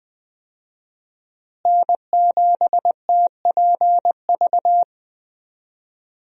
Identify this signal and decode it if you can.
Morse code, heard as a keyed tone, about 21 wpm, 700 Hz. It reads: N7TPV
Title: USA Call Signs - 5 Characters - Repeated 2x - 20wpm